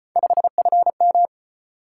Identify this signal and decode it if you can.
Morse code, heard as a keyed tone, about 34 wpm, 700 Hz. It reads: 5FM